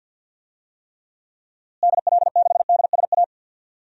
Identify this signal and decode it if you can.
Morse code, heard as a keyed tone, about 52 wpm, 700 Hz. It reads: DL6DSA
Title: DXCC-100 Call Signs - 6 Characters - Repeated 2x - 50wpm